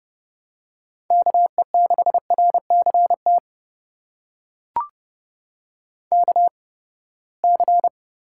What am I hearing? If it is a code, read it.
Morse code, heard as a keyed tone, about 30 wpm, spaced as for 21 wpm, 700 Hz. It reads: KE6RCT KC
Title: USA Call Signs - 6 Characters - 8x Character Spacing - 30wpm